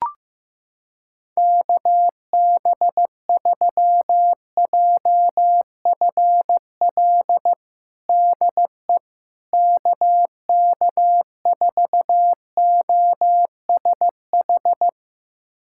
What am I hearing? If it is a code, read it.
Morse code, heard as a keyed tone, about 15 wpm, 700 Hz. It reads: KB3JFL DE KK4OSH
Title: USA Callsign DE Callsign 15wpm